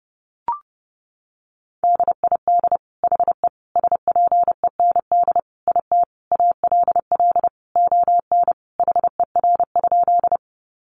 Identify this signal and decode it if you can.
Morse code, heard as a keyed tone, about 30 wpm, 700 Hz. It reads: DID HE SPEND IT ALL ON HER?
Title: Sentences from Top 800 Words 30wpm